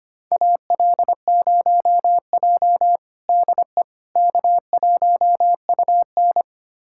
Morse code, heard as a keyed tone, about 25 wpm, 700 Hz. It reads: AL0J DE K1UN
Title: USA Callsign DE Callsign - 4 Characters - 25wpm